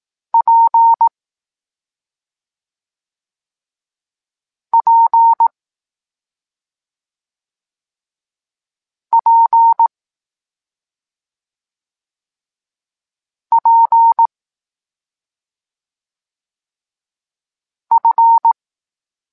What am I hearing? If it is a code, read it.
Morse code, heard as a keyed tone, about 18 wpm, spaced as for 2 wpm, 920 Hz. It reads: PPPPF
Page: Tekst 25 – Nauka liter P F